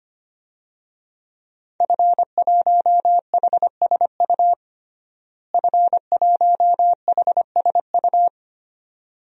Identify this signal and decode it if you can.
Morse code, heard as a keyed tone, about 25 wpm, 700 Hz. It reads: F1HSU F1HSU
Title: DXCC-100 Call Signs - 5 Characters - Repeated 2x - 25wpm